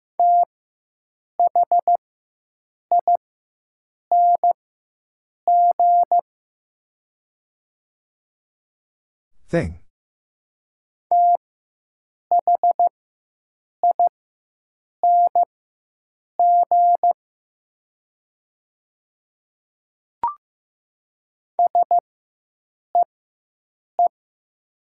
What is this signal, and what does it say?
Morse code, heard as a keyed tone, about 15 wpm, spaced as for 7 wpm, 700 Hz. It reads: THING THING SEE
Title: 4x Character Spacing - Top 100 Words - 15wpm